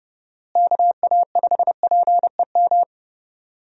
Morse code, heard as a keyed tone, about 32 wpm, 700 Hz. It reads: KA5PEM
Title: USA Call Signs - 6 Characters - Repeated 2x - 30wpm